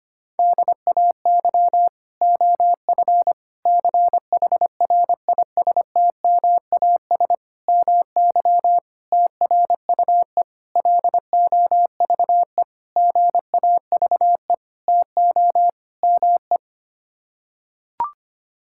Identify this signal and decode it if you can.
Morse code, heard as a keyed tone, about 25 wpm, 700 Hz. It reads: DAY OF CHRISTMAS MY TRUE LOVE GAVE TO ME
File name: Sets of 9 Words - 25wpm